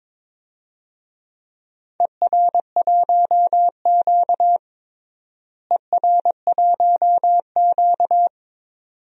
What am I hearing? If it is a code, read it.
Morse code, heard as a keyed tone, about 22 wpm, 700 Hz. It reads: ER1Q ER1Q
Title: DXCC-100 Call Signs - 4 Characters - Repeated 2x - 22wpm